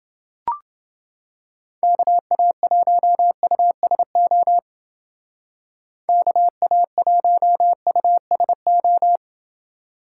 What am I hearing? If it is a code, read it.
Morse code, heard as a keyed tone, about 29 wpm, 700 Hz. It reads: KA1USO KA1USO